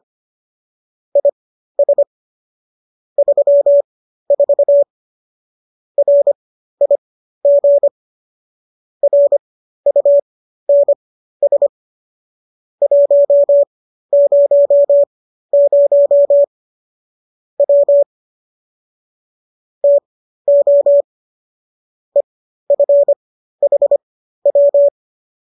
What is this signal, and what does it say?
Morse code, heard as a keyed tone, about 25 wpm, spaced as for 13 wpm, 570 Hz. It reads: IS 34 RIG RUNS 100 W TO EFHW
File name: qso207 13